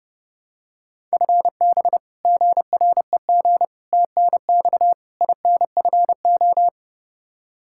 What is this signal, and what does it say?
Morse code, heard as a keyed tone, about 30 wpm, 700 Hz. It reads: FB GREG TNX INFO
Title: Practice QSO - Exchange 3 - More Info - 30wpm